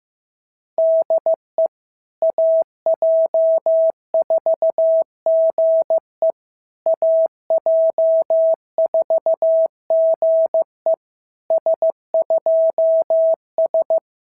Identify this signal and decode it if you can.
Morse code, heard as a keyed tone, about 15 wpm, 650 Hz. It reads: DE AJ4GE AJ4GE S2S
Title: SOTA QSO Practice 15wpm